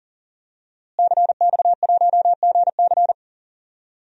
Morse code, heard as a keyed tone, about 40 wpm, 700 Hz. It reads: CX1GC
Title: DXCC-100 Call Signs - 5 Characters - Repeated 3x - 40wpm